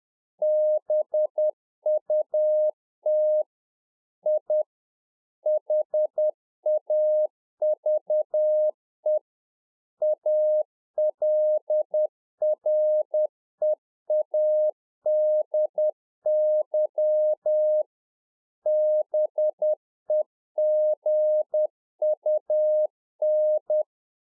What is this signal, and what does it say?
Morse code, heard as a keyed tone, about 10 wpm, 610 Hz. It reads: BUT I HAVE ALREADY BEGUN